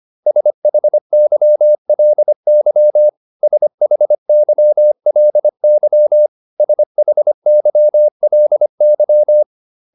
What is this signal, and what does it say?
Morse code, heard as a keyed tone, about 25 wpm, 600 Hz. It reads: SHYLY SHYLY SHYLY